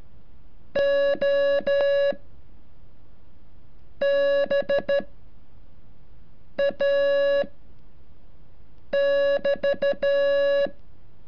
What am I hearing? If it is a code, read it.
Morse code, heard as a keyed tone, about 13 wpm, 560 Hz. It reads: O B A <BT>
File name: curso / ejercicio 33